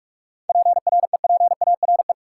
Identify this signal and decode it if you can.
Morse code, heard as a keyed tone, about 46 wpm, 700 Hz. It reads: PREPARE
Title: Sentences from Top 900 Words - New Word Drill - 45wpm